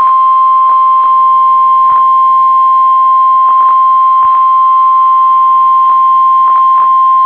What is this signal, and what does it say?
Emergency Alert System (SAME) tones (the 1050 Hz attention tone). It is heard from 0.3 to 6.2 s.